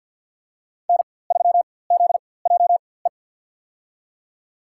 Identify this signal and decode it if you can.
Morse code, heard as a keyed tone, about 51 wpm, spaced as for 23 wpm, 700 Hz. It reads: N3ZJE
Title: USA Call Signs - 5 Characters - 4x Character Spacing - 50wpm